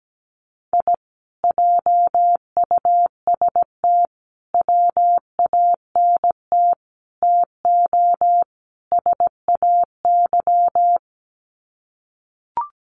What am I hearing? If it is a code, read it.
Morse code, heard as a keyed tone, about 17 wpm, 700 Hz. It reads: I JUST WANT TO SAY
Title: Sets of 5 Words Encore 17wpm